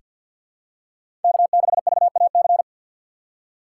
Morse code, heard as a keyed tone, about 48 wpm, 700 Hz. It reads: K6VAC
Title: USA Call Signs - 2x Character Spacing - 50wpm